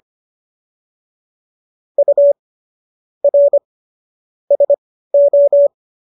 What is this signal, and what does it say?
Morse code, heard as a keyed tone, about 25 wpm, spaced as for 15 wpm, 570 Hz. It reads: U R SO